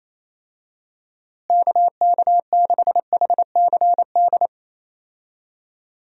Morse code, heard as a keyed tone, about 28 wpm, 700 Hz. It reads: KK6HCD
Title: USA Call Signs - 6 Characters - Repeated 3x - 28wpm